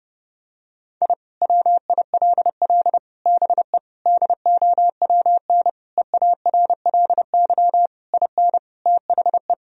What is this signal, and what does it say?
Morse code, heard as a keyed tone, about 29 wpm, 700 Hz. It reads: I WILL BE DOWN EARLY IN THE
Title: Sentences from Top 1000 Words - Review Part 2 - 30wpm